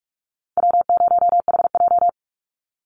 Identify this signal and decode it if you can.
Morse code, heard as a keyed tone, about 44 wpm, 700 Hz. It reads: W0HJ
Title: USA Call Signs 45wpm